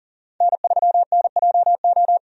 Morse code, heard as a keyed tone, about 40 wpm, 700 Hz. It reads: N3NJO
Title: USA Call Signs - 5 Characters - 40wpm